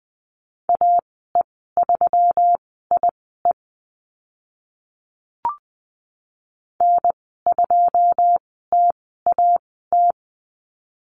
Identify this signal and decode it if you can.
Morse code, heard as a keyed tone, about 20 wpm, spaced as for 15 wpm, 700 Hz. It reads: AE3IE N2TAT
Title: 147 Lesson 24g - Letter y - Call Signs 20wpm